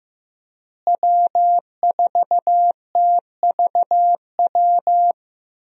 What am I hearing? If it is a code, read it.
Morse code, heard as a keyed tone, about 16 wpm, 700 Hz. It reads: W4TVW
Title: USA Call Signs - 5 Characters - Repeated 3x - 15wpm